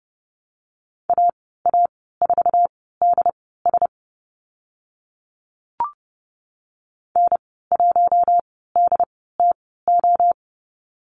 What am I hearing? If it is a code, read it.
Morse code, heard as a keyed tone, about 30 wpm, spaced as for 17 wpm, 700 Hz. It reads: AA4DS N1DTO
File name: Lesson 14g - Number 2 - Call Signs 30wpm